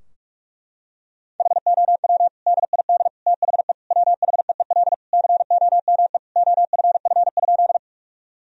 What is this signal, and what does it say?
Morse code, heard as a keyed tone, about 46 wpm, 700 Hz. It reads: HOW DID THE WHEEL COME OFF?
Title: Sentences from Top 1000 Words - Review Part 3 - 45wpm